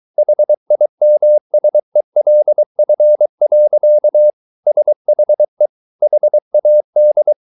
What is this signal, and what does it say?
Morse code, heard as a keyed tone, about 23 wpm, 600 Hz. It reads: HIMSELF. SHE HAD